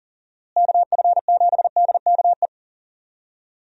Morse code, heard as a keyed tone, about 40 wpm, 700 Hz. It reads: KF7DKE
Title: USA Call Signs - 6 Characters - Repeated 3x - 40wpm